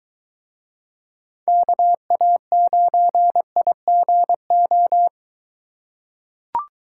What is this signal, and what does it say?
Morse code, heard as a keyed tone, about 23 wpm, 700 Hz. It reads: KA9IGO